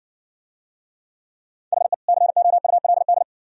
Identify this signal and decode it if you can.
Morse code, heard as a keyed tone, about 60 wpm, 690 Hz. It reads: HE COULD